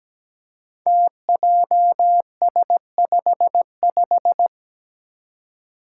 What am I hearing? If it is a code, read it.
Morse code, heard as a keyed tone, about 17 wpm, 700 Hz. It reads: TJS55
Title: Five Challenging Letter-Number Combinations 17wpm